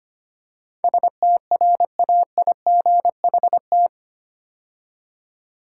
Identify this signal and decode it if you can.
Morse code, heard as a keyed tone, about 25 wpm, 690 Hz. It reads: STRAIGHT
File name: 8 Letter Words - First 3 Letters Given - 25wpm